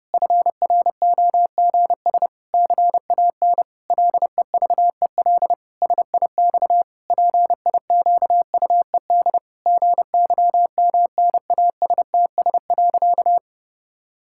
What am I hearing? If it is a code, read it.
Morse code, heard as a keyed tone, about 32 wpm, 700 Hz. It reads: FROGS CAN LEVEL SIX PIQUED GYMNASTS.
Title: Pangram Challenge v2 - 30wpm